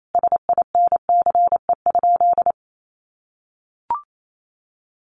Morse code, heard as a keyed tone, about 28 wpm, 700 Hz. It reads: SINCE?
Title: Sentences from Top 700 Words 28wpm